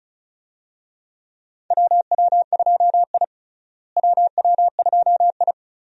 Morse code, heard as a keyed tone, about 34 wpm, 700 Hz. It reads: WW2I WW2I
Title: USA Call Signs - 4 Characters - Repeated 2x - 35wpm